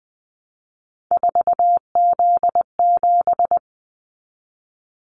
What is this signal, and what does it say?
Morse code, heard as a keyed tone, about 19 wpm, 700 Hz. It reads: 4Z7
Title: Callsign Prefixes for Top 50 DXCC - 20wpm